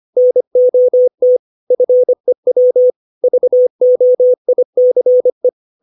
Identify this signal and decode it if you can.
Morse code, heard as a keyed tone, about 25 wpm, 500 Hz. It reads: NOT FEW VOICE
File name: 632 words / 632 Words 25WPM